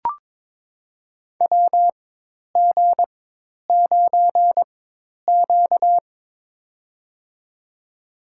Morse code, heard as a keyed tone, about 22 wpm, spaced as for 10 wpm, 700 Hz. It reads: WG9Q
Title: USA Call Signs - 4 Characters - 4x Character Spacing - 22wpm